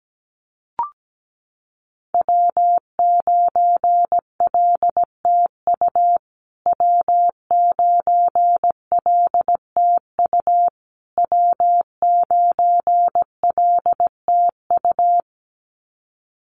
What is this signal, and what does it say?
Morse code, heard as a keyed tone, about 17 wpm, 700 Hz. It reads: W9LTU W9LTU W9LTU
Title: USA Call Signs Repeated 3x 17wpm